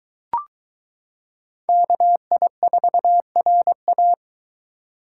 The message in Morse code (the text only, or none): KI4RA